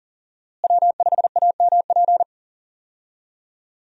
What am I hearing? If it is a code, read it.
Morse code, heard as a keyed tone, about 40 wpm, 700 Hz. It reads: W5AMP
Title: USA Call Signs - 5 Characters - Repeated 2x - 40wpm